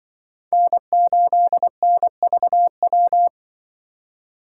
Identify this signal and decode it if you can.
Morse code, heard as a keyed tone, about 23 wpm, 700 Hz. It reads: N8NVW
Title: Speed Racing - USA 1x3 Call Signs 20wpm